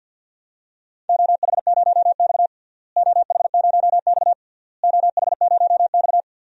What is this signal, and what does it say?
Morse code, heard as a keyed tone, about 48 wpm, 700 Hz. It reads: OH0X OH0X OH0X
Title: DXCC-100 Call Signs - 4 Characters - Repeated 3x - 50wpm